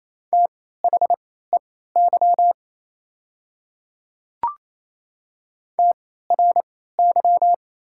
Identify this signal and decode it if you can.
Morse code, heard as a keyed tone, about 28 wpm, spaced as for 16 wpm, 690 Hz. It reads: THEY TRY
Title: 3x Character Spacing - Top 200 Words - 28wpm